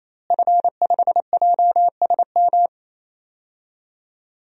Morse code, heard as a keyed tone, about 28 wpm, 700 Hz. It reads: F5JSM
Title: DXCC-100 Call Signs - 5 Characters - Repeated 3x - 28wpm